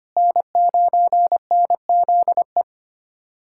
Morse code, heard as a keyed tone, about 25 wpm, 700 Hz. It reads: N9NZE
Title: USA Callsign DE Callsign - 5 Characters - 25wpm